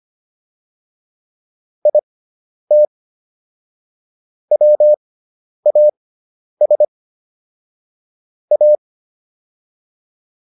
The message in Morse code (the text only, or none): IT WAS A